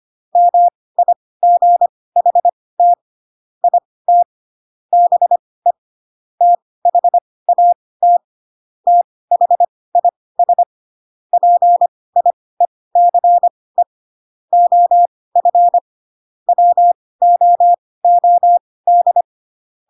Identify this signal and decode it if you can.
Morse code, heard as a keyed tone, about 25 wpm, spaced as for 18 wpm, 700 Hz. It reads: MIGHT IT BE THAT THIS PIECE OF WOOD
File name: Pinocchio / Pinocchio ch1 12 25 ik1hge